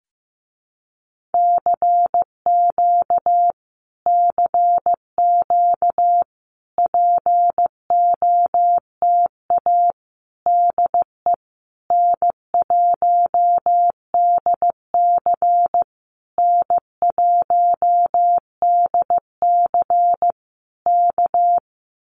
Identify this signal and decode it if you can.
Morse code, heard as a keyed tone, about 15 wpm, 700 Hz. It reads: CQ CQ POTA DE N1DC N1DC K
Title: Common QSO Phrases Encore 15wpm